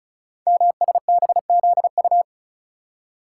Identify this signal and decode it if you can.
Morse code, heard as a keyed tone, about 36 wpm, 690 Hz. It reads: MSBZU
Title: Code Groups - 5 Letters - 35wpm